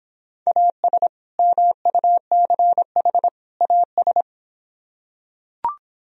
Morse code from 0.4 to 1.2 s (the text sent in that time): AS